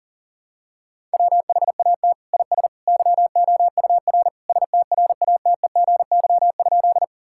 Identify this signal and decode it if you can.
Morse code, heard as a keyed tone, about 40 wpm, 690 Hz. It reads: WHAT IS YOUR STRATEGY?
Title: Sentences from Top 1000 Words - Review Part 5 - 40wpm